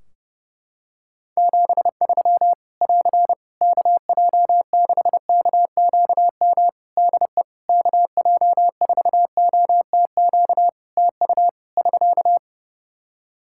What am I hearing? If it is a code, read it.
Morse code, heard as a keyed tone, about 30 wpm, 700 Hz. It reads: 73 <AR> KJ6KQM DE KJ4OTQ TU <SK>
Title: Practice QSO - Exchange 4 - Ending - 30wpm